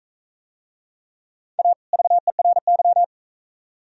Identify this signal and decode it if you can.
Morse code, heard as a keyed tone, about 37 wpm, 700 Hz. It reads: A VERY